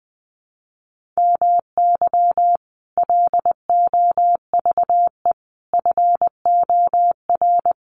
Morse code, heard as a keyed tone, about 20 wpm, 700 Hz. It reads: MY LOVE FOR
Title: Sentences from Top 1400 Words 20wpm